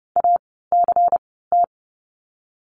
Morse code, heard as a keyed tone, about 30 wpm, spaced as for 17 wpm, 700 Hz. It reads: ACT
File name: Lesson 16f - Letter c - Words 30wpm